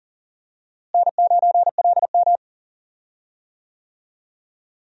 Morse code, heard as a keyed tone, about 40 wpm, 700 Hz. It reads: N9LM
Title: USA Call Signs - 4 Characters - Repeated 3x - 40wpm